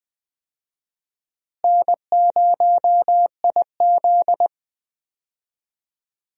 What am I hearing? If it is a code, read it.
Morse code, heard as a keyed tone, about 20 wpm, 700 Hz. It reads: N0IZ
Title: USA Call Signs - 4 Characters - Repeated 3x - 20wpm